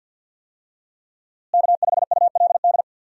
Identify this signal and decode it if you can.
Morse code, heard as a keyed tone, about 51 wpm, 700 Hz. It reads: K5ULD
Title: USA Call Signs - 5 Characters - Repeated 2x - 50wpm